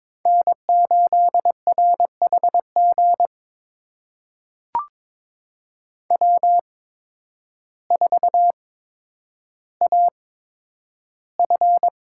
Morse code, heard as a keyed tone, about 22 wpm, spaced as for 10 wpm, 700 Hz. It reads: N8RHG W4AF
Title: USA Call Signs - 5 Characters - 8x Character Spacing - 22wpm